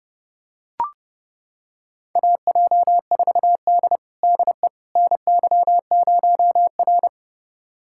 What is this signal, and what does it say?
Morse code, heard as a keyed tone, about 32 wpm, 700 Hz. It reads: AJ4D DE NY0R
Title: USA Callsign DE Callsign - 4 Characters - 30wpm